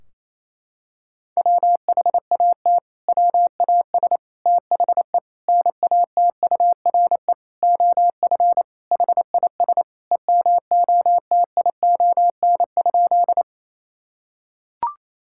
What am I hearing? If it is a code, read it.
Morse code, heard as a keyed tone, about 28 wpm, 700 Hz. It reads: WHAT WAS THE NATURE OF HIS EMOTION?
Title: Sentences from Top 1600 Words 28wpm